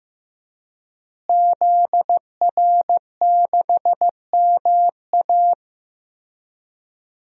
Morse code, heard as a keyed tone, about 14 wpm, 700 Hz. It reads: ZR6MA
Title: DXCC-100 Call Signs - 5 Characters - Repeated 3x - 15wpm